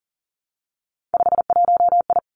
Morse code, heard as a keyed tone, about 42 wpm, 700 Hz. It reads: 51I
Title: Three Challenging Letter-Number Combinations 40wpm